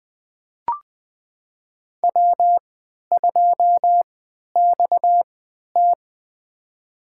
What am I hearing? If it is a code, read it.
Morse code, heard as a keyed tone, about 21 wpm, spaced as for 12 wpm, 700 Hz. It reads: W2XT
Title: USA Call Signs - 4 Characters - 3x Character Spacing - 20wpm